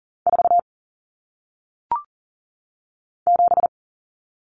Morse code, heard as a keyed tone, about 40 wpm, 700 Hz. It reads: 4 7
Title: Common Cut Numbers - 1-Digit 40wpm